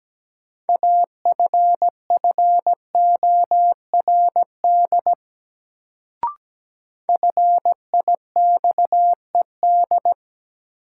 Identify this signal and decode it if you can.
Morse code, heard as a keyed tone, about 17 wpm, 700 Hz. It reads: AFFORD FIXED
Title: Sentences from Top 1100 Words - New Word Drill - 17wpm